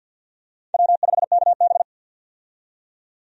Morse code, heard as a keyed tone, about 53 wpm, 700 Hz. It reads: W5KB
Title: USA Call Signs - 4 Characters - 50wpm